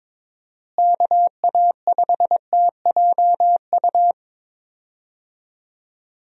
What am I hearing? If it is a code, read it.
Morse code, heard as a keyed tone, about 22 wpm, 700 Hz. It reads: KA5TJU
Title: USA Call Signs - 6 Characters - Repeated 2x - 22wpm